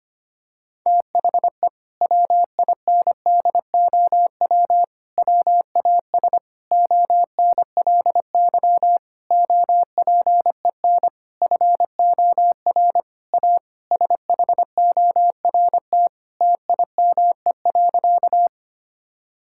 Morse code, heard as a keyed tone, about 25 wpm, 700 Hz. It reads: THE WINDOW WAS ONLY OPEN FOR A SHORT TIME.
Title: Sentences from Top 1600 Words 25wpm